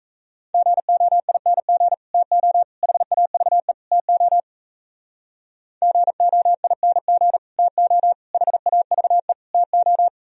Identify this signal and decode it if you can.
Morse code, heard as a keyed tone, about 34 wpm, 700 Hz. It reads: GOING TO HAVE TO GOING TO HAVE TO
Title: Speed-Racing - Sets of 4 Words 28wpm